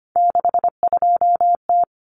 Morse code, heard as a keyed tone, about 25 wpm, 700 Hz. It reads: 62T
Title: Common Cut Numbers - 3-Digit 25wpm